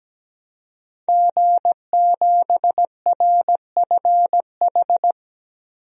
Morse code, heard as a keyed tone, about 17 wpm, 700 Hz. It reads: G7RFH
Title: DXCC-100 Call Signs - 5 Characters - Repeated 2x - 17wpm